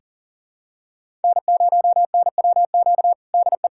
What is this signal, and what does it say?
Morse code, heard as a keyed tone, about 37 wpm, 700 Hz. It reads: N0NWQ DE
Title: USA Callsign DE Callsign - 5 Characters - 40wpm